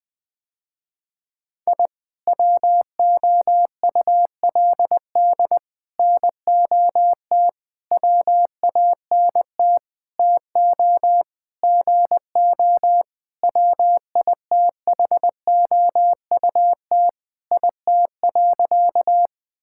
Morse code, heard as a keyed tone, about 20 wpm, 700 Hz. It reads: I WOULD NOT WANT TO GO WITHOUT IT.
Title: Sentences with 5-Word Phrases - Spoken Beginning - 20wpm